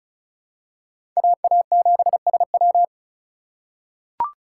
Morse code, heard as a keyed tone, about 36 wpm, 700 Hz. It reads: AA7SW
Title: USA Call Signs - 5 Characters - 2x Character Spacing - 35wpm